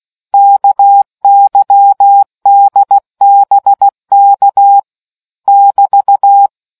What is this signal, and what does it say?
Morse code, heard as a keyed tone, about 16 wpm, 800 Hz. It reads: KYDBK <BT>